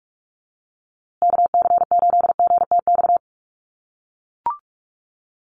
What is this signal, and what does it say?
Morse code, heard as a keyed tone, about 44 wpm, 700 Hz. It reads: KC8GTX
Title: USA Call Signs Repeated 2x 45wpm